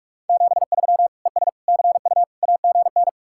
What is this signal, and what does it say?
Morse code, heard as a keyed tone, about 66 wpm, spaced as for 54 wpm, 700 Hz. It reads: 73 ES CU AGN
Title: Speed-Racing Common QSO Phrases - 45wpm